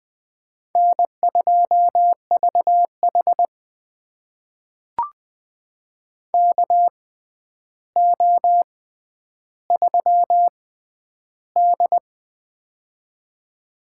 Morse code, heard as a keyed tone, about 20 wpm, spaced as for 10 wpm, 700 Hz. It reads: N2VH KO3D
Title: USA Call Signs - 4 Characters - 6x Character Spacing - 20wpm